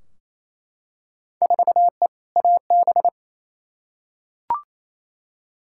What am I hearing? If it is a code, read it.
Morse code, heard as a keyed tone, about 28 wpm, 700 Hz. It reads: 4E AB
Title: Field Day Exchanges Encore - 28wpm